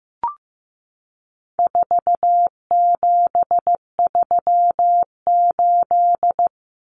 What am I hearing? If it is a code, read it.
Morse code, heard as a keyed tone, about 15 wpm, 700 Hz. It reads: 4738
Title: Numbers - 4-Digit 15wpm